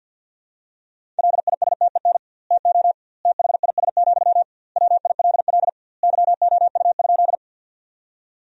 Morse code, heard as a keyed tone, about 50 wpm, 700 Hz. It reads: LISTEN TO THIS, WILL YOU?
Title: 2x Word-Spacing - Sentences from Top 400 Words - 50wpm